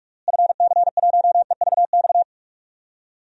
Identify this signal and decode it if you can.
Morse code, heard as a keyed tone, about 48 wpm, 700 Hz. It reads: FC1EVX
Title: DXCC-100 Call Signs - 6 Characters - Repeated 3x - 45wpm